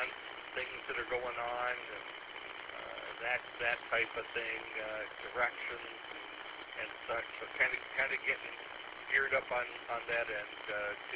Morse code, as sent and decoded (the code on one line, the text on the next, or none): none